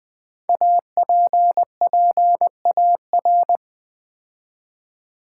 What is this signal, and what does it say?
Morse code, heard as a keyed tone, about 20 wpm, 700 Hz. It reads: APPAR
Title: Build A Word - 5 Letters At A Time - 10-Letter Words - 20wpm